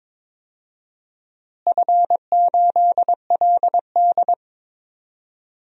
Morse code, heard as a keyed tone, about 22 wpm, 700 Hz. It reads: F8LD
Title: DXCC-100 Call Signs - 4 Characters - Repeated 2x - 22wpm